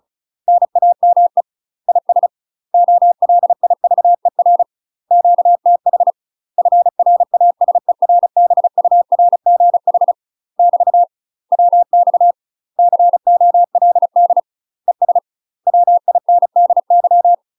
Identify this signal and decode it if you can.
Morse code, heard as a keyed tone, about 37 wpm, 700 Hz. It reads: NAME IS OLIVER QTH FRASERBURGH <BT> WX COLD ES WINDY
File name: UK MASTER QSOs TRAINING / 35WPM / UK MASTER QSOs 0043